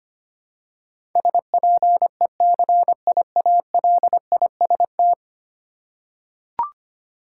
Morse code, heard as a keyed tone, about 25 wpm, 700 Hz. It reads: SPECIALIST
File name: 10 Letter Words - First 3 Letters Given - 25wpm